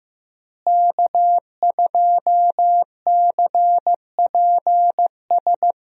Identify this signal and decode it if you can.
Morse code, heard as a keyed tone, about 15 wpm, 700 Hz. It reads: K2CPS
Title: USA Call Signs - 5 Characters - Repeated 3x - 15wpm